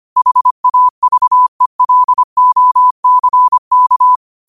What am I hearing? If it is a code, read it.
Morse code, heard as a keyed tone, about 25 wpm, 1000 Hz. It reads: HAVELOCK